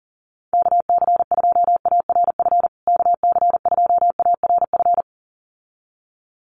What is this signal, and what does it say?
Morse code, heard as a keyed tone, about 38 wpm, 700 Hz. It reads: KC2ARF KC2ARF
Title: USA Call Signs Repeated 2x 40wpm